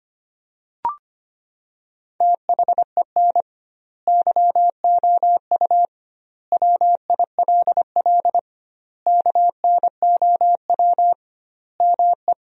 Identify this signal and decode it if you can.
Morse code, heard as a keyed tone, about 25 wpm, 700 Hz. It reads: THEN YOU WILL KNOW ME
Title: 2x Word-Spacing - Sentences from Top 400 Words - 25wpm